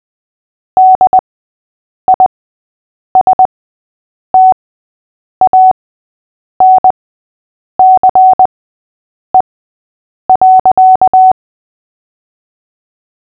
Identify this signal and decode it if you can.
Morse code, heard as a keyed tone, about 20 wpm, spaced as for 8 wpm, 750 Hz. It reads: DISTANCE.
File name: cw / fists / Short-Story-6-8wpm